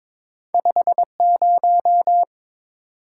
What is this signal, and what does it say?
Morse code, heard as a keyed tone, about 30 wpm, spaced as for 26 wpm, 700 Hz. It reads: 50
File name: Speed-Racing - Two Letter-Number Combinations - 22wpm